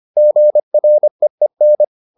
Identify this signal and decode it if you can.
Morse code, heard as a keyed tone, about 25 wpm, 600 Hz. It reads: GREEN